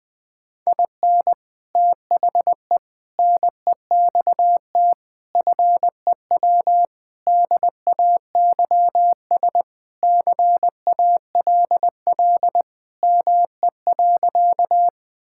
Morse code, heard as a keyed tone, about 20 wpm, 700 Hz. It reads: IN THE NEXT FEW DAYS CALL ME.
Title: Sentences with 3-Word Phrases - Spoken Beginning - 20wpm